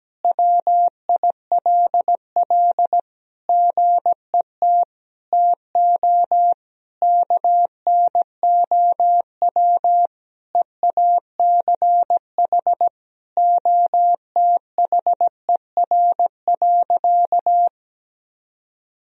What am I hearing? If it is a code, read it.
Morse code, heard as a keyed tone, about 17 wpm, 700 Hz. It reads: WILL GET TO KNOW EACH OTHER.
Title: Sentences from Top 1000 Words - Review Part 3 - 17wpm